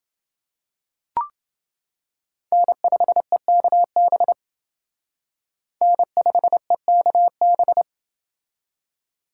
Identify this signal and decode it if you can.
Morse code, heard as a keyed tone, about 28 wpm, 700 Hz. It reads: N5EKB N5EKB